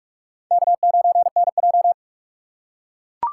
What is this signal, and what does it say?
Morse code, heard as a keyed tone, about 44 wpm, 700 Hz. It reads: K9NJ